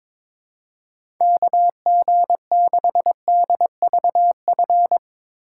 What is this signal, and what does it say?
Morse code, heard as a keyed tone, about 22 wpm, 700 Hz. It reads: KG6DVF
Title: USA Call Signs - 3x Character Spacing - 22wpm